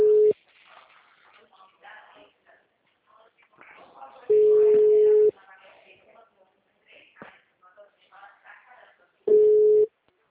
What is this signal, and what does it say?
Ringback tone (Europe, 425 Hz), with speech.
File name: out- -2006- - - .27343